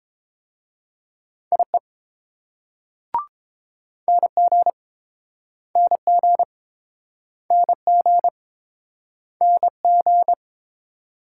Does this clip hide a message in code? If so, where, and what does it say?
Yes: Morse code from 1.5 to 10.3 s, heard as a keyed tone, about 29 wpm, 700 Hz. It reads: IE NG NG NG NG